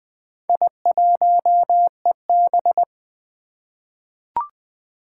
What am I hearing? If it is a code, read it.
Morse code, heard as a keyed tone, about 20 wpm, 700 Hz. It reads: I1EB
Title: Four Challenging Letter-Number Combinations 20wpm